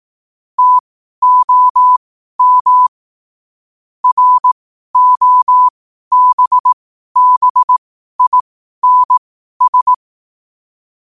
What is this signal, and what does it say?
Morse code, heard as a keyed tone, about 18 wpm, spaced as for 12 wpm, 1000 Hz. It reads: TOM ROBBINS
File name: qotdfiles / 12wpm